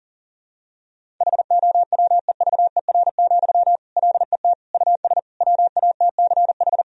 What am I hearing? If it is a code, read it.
Morse code, heard as a keyed tone, about 40 wpm, 700 Hz. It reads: HOWEVER, LET US WATCH
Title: Sentences from Top 1000 Words - Review Part 1 - 40wpm